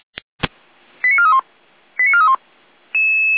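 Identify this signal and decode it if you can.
Five-tone selective call (ZVEI), heard as quick runs of five tones, tones of about 70 ms. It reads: 89431 89431
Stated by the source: ZVEI five-tone selective call received on an SDR.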